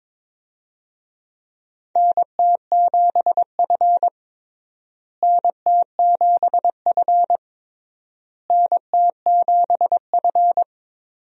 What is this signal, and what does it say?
Morse code, heard as a keyed tone, about 22 wpm, 700 Hz. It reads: NT7F NT7F NT7F
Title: USA Call Signs - 4 Characters - Repeated 3x - 22wpm